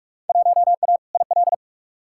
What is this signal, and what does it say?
Morse code, heard as a keyed tone, about 46 wpm, 700 Hz. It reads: 1A IL
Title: Field Day Exchanges Encore - 45wpm